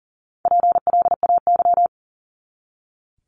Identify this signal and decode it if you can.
Morse code, heard as a keyed tone, about 40 wpm, 700 Hz. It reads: PLAY
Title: ING Suffix - Top 100 Words with Base Words 40wpm